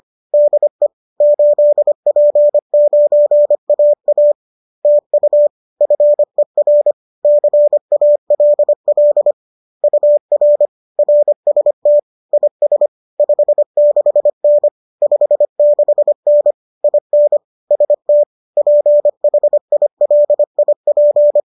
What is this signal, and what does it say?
Morse code, heard as a keyed tone, about 25 wpm, 590 Hz. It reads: DE 8P9AA TU FER CALL UR RST IS 56N 56N IN ST PHILIP